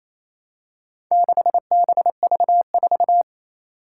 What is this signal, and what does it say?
Morse code, heard as a keyed tone, about 27 wpm, 700 Hz. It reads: 6BV4
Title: Four Challenging Letter-Number Combinations 28wpm